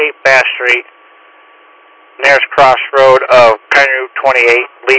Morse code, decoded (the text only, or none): none